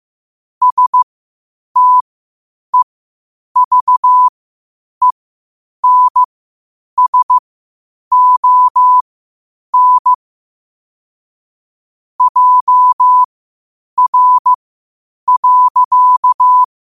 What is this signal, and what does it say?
Morse code, heard as a keyed tone, about 15 wpm, spaced as for 8 wpm, 1000 Hz. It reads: STEVENSON JR.
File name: qotdfiles / 8wpm